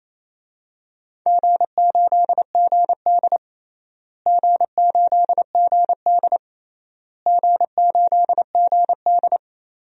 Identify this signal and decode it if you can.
Morse code, heard as a keyed tone, about 28 wpm, 700 Hz. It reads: G8GD G8GD G8GD
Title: DXCC-100 Call Signs - 4 Characters - Repeated 3x - 28wpm